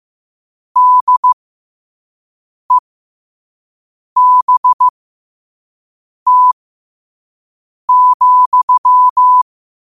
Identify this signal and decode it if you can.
Morse code, heard as a keyed tone, about 15 wpm, spaced as for 5 wpm, 1000 Hz. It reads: DEBT,
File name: qotdfiles / 5wpm